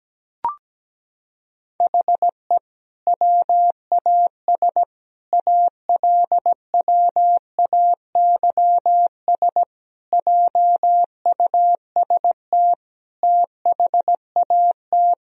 Morse code, heard as a keyed tone, about 17 wpm, 700 Hz. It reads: HE WAS ALWAYS JUST THAT
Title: Sentences from Top 1000 Words - Review Part 4 - 17wpm